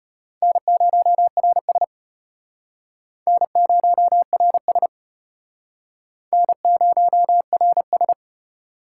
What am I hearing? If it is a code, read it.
Morse code, heard as a keyed tone, about 33 wpm, 700 Hz. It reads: N0RS N0RS N0RS